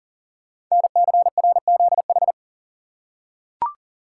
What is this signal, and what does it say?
Morse code, heard as a keyed tone, about 42 wpm, 690 Hz. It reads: NCRZH
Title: Code Groups - 5 Letters - 40wpm